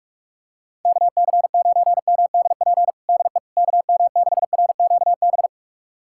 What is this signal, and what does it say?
Morse code, heard as a keyed tone, about 42 wpm, 700 Hz. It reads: KC9MDP DE KM6RQB
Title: USA Callsign DE Callsign - 6 Characters - 45wpm